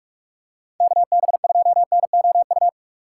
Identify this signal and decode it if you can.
Morse code, heard as a keyed tone, about 46 wpm, 700 Hz. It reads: KB2NOU
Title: USA Call Signs - 6 Characters - 3x Character Spacing - 45wpm